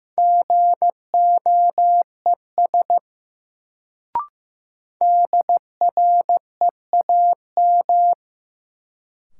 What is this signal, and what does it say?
Morse code, heard as a keyed tone, about 15 wpm, 700 Hz. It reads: GOES DREAM
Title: Sentences from Top 1500 Words - New Word Drill Alt - 15wpm